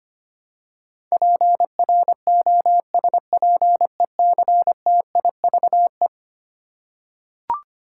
Morse code, heard as a keyed tone, about 25 wpm, 700 Hz. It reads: PROSPECTIVE
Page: morse-code-ninja-practice-25wpm